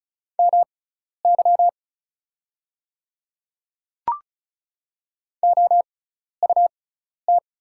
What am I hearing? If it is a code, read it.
Morse code, heard as a keyed tone, about 36 wpm, spaced as for 12 wpm, 700 Hz. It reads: MY OUT
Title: 6x Character Spacing - Top 100 Words - 35wpm